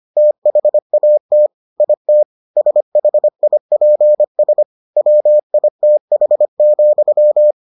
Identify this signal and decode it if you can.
Morse code, heard as a keyed tone, about 25 wpm, 600 Hz. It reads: THAT IT SHIPS WITH,